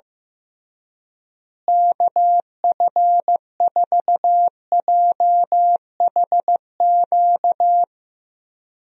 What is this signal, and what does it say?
Morse code, heard as a keyed tone, about 16 wpm, 700 Hz. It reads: KF4JHQ
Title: USA Call Signs - 6 Characters - Repeated 3x - 15wpm